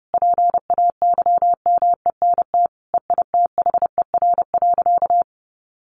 Morse code, heard as a keyed tone, about 28 wpm, 700 Hz. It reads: PAYMENT EITHER.
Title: Sentences from Top 1200 Words 30wpm